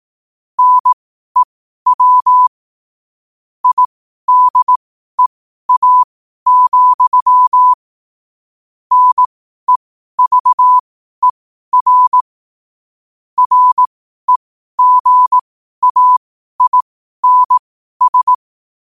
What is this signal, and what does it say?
Morse code, heard as a keyed tone, about 18 wpm, spaced as for 12 wpm, 1000 Hz. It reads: NEW IDEA, NEVER REGAINS
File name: qotdfiles / 12wpm